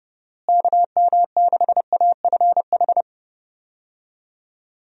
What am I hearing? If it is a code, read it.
Morse code, heard as a keyed tone, about 32 wpm, 700 Hz. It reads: KM6AFH
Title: USA Call Signs - 6 Characters - Repeated 2x - 30wpm